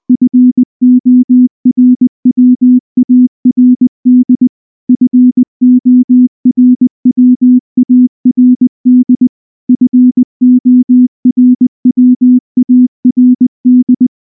Morse code, heard as a keyed tone, about 19 wpm, 250 Hz. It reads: FORWARD FORWARD FORWARD
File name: morse / morse-training-20WPM-2